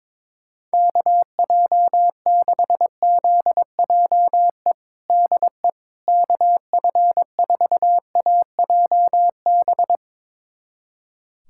Morse code, heard as a keyed tone, about 22 wpm, 700 Hz. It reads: KJ6ZJE DE KF4AJB
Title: USA Callsign DE Callsign 22wpm